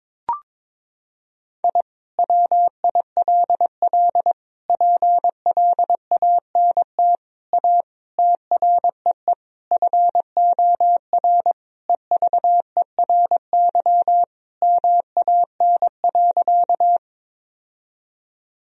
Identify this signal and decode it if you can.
Morse code, heard as a keyed tone, about 22 wpm, 700 Hz. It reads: I WILL PLANT A TREE FOR EVERY MAN.
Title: Sentences from Top 1000 Words - Review Part 1 - 22wpm